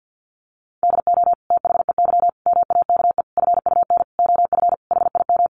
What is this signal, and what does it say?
Morse code, heard as a keyed tone, about 50 wpm, 700 Hz. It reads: DO THEY MAKE FUN OF HIM
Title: Sentences from Top 1300 Words 50wpm